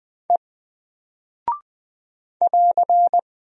Morse code, heard as a keyed tone, about 20 wpm, 700 Hz. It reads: E <AR>
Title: Single Character (Letter:Number:Punctuation:Prosign) Alt 20wpm